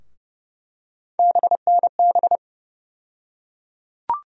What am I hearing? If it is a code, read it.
Morse code, heard as a keyed tone, about 30 wpm, 700 Hz. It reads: BNB
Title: USA Call Sign Postfixes - 3 Letters 30wpm